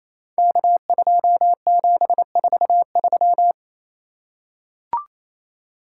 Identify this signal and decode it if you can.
Morse code, heard as a keyed tone, about 29 wpm, 700 Hz. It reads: K2743
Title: Code Groups - 5 Letter:Numbers - 28wpm